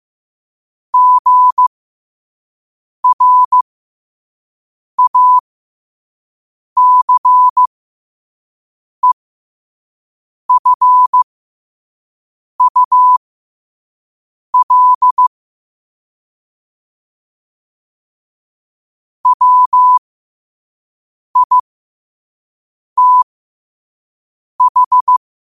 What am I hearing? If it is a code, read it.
Morse code, heard as a keyed tone, about 15 wpm, spaced as for 5 wpm, 1000 Hz. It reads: GRACEFUL WITH
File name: qotdfiles / 5wpm